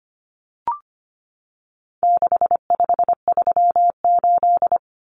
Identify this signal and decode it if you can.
Morse code, heard as a keyed tone, about 25 wpm, 700 Hz. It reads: 6538
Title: Common Cut Numbers - 4-Digit 25wpm